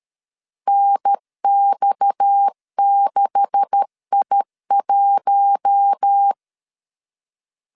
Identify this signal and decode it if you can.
Morse code, heard as a keyed tone, about 13 wpm, 800 Hz. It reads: NX6I1
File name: CW / DET / 1 9 1500C S A 12wpm 3800